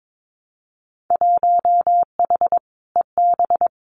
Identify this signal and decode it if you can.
Morse code, heard as a keyed tone, about 22 wpm, 700 Hz. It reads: 1H EB
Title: Winter Field Day Exchanges 22wpm